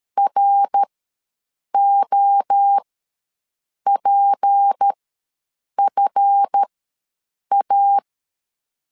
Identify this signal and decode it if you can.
Morse code, heard as a keyed tone, about 13 wpm, spaced as for 7 wpm, 800 Hz. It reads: ROPFA
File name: LEZIONI / 10 SIGNS / 10D COMA ws 12wpm 3618